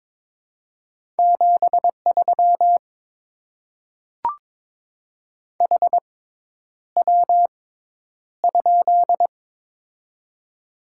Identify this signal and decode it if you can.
Morse code, heard as a keyed tone, about 22 wpm, spaced as for 10 wpm, 700 Hz. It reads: 73 HW?
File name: Top 20 QSO Elements - 6x Character Spacing - 22wpm